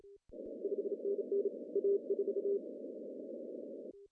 Morse code, heard as a keyed tone, about 31 wpm, 400 Hz. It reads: 5NN A4
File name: CW / cqww-cw-k5tr